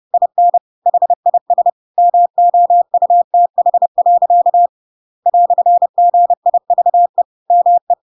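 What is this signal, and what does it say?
Morse code, heard as a keyed tone, about 30 wpm, 700 Hz. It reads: IN HIS MOUTH. "GIVE ME
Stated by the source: Chapter 2 of The Adventures of Pinocchio read as a book in Morse code (CW) at 30 wpm, by ik1hge.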